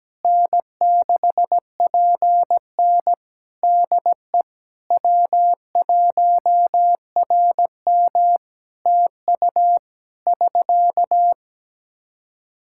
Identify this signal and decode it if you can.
Morse code, heard as a keyed tone, about 17 wpm, 700 Hz. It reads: N6PN DE W1RM TU <SK>
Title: Practice QSO - Exchange 4 - Ending - 17wpm